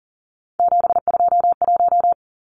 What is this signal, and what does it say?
Morse code, heard as a keyed tone, about 40 wpm, 700 Hz. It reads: 721
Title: Numbers - 3-Digit 40wpm